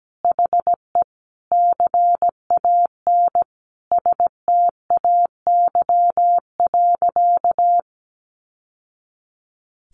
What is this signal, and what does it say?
Morse code, heard as a keyed tone, about 17 wpm, 700 Hz. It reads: HE CAN STAY.
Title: Sentences from Top 1000 Words 17wpm